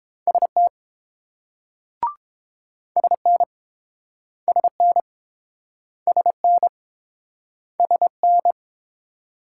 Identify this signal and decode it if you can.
Morse code, heard as a keyed tone, about 31 wpm, 700 Hz. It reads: ST SN SN SN SN